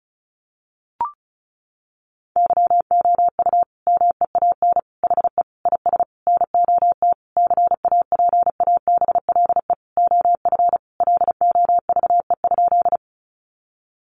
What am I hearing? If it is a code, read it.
Morse code, heard as a keyed tone, about 34 wpm, 700 Hz. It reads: YOU MEAN HE IS NOT CAPABLE OF LOVE?
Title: Sentences from Top 1200 Words 35wpm